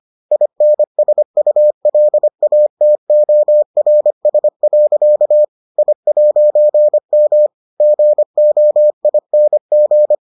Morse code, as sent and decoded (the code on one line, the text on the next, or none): .. -. ... ..- .-.. .- - --- .-. ... .-.-.- / .. .----. -- / --. --- .. -. --.
INSULATORS. I'M GOING